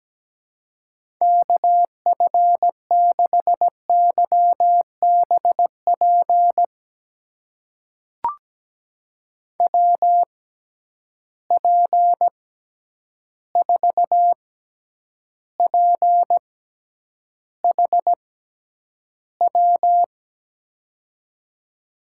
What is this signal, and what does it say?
Morse code, heard as a keyed tone, about 17 wpm, spaced as for 9 wpm, 700 Hz. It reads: KF6YBP WP4PHW
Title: USA Call Signs - 6 Characters - 6x Character Spacing - 17wpm